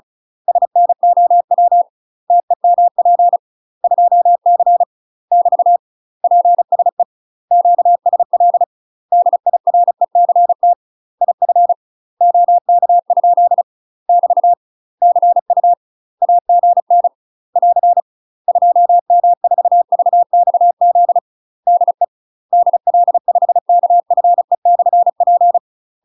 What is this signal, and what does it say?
Morse code, heard as a keyed tone, about 35 wpm, 700 Hz. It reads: SNOW TEMP 2C <BT> PSE QSL DIRECT IF OK? <BT> CU AGN <AR> 2M4VXZ DE DL5KFE/P
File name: UK MASTER QSOs TRAINING / 35WPM / UK MASTER QSOs 0032